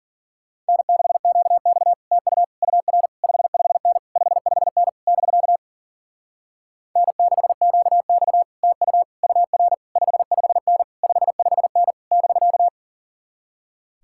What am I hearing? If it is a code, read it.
Morse code, heard as a keyed tone, about 48 wpm, 700 Hz. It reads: N6QX TU UR 55N 55N <BK> N6QX TU UR 55N 55N <BK>
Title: Speed-Racing Common QSO Phrases - 40wpm